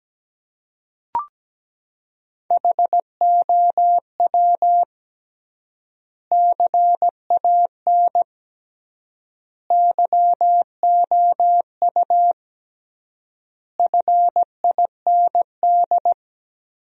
Morse code, heard as a keyed tone, about 17 wpm, 700 Hz. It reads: HOW CAN YOU FIND
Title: 3x Word-Spacing - Sentences from Top 200 Words - 17wpm